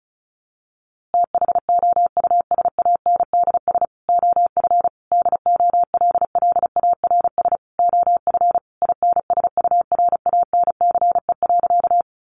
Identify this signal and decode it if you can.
Morse code, heard as a keyed tone, about 34 wpm, 700 Hz. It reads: THOUSANDS OF DOLLARS OF INSURANCE.
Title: A Sound Of Thunder 35wpm